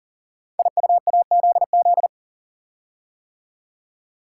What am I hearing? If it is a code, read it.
Morse code, heard as a keyed tone, about 38 wpm, 700 Hz. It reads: IUAZZ
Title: Code Groups - 5 Letters - 40wpm